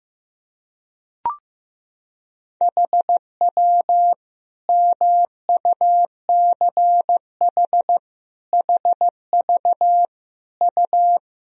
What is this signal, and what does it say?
Morse code, heard as a keyed tone, about 15 wpm, 700 Hz. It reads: HW MUCH HV U
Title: Sentences from Top 300 Words and Abbreviations 15wpm